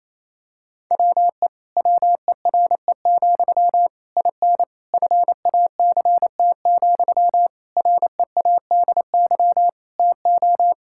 Morse code, heard as a keyed tone, about 28 wpm, 700 Hz. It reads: WE WERE, IN FACT, READY TO
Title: Sentences from Top 1000 Words - Review Part 5 - 28wpm